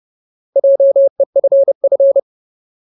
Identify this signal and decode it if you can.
Morse code, heard as a keyed tone, about 30 wpm, 550 Hz. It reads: JEFF
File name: Ultimate Word List / 30WPM Ultimate